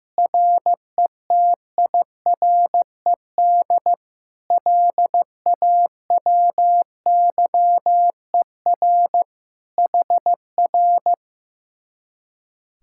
Morse code, heard as a keyed tone, about 16 wpm, 700 Hz. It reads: RETIRED LAWYER HR
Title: Speed-Racing Common QSO Phrases Encore - 15wpm